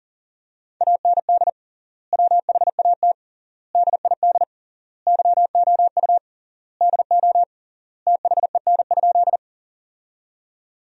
Morse code, heard as a keyed tone, about 40 wpm, 700 Hz. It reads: AND WHAT DID YOU DO THEN?
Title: 3x Word-Spacing - Sentences from Top 100 Words - 40wpm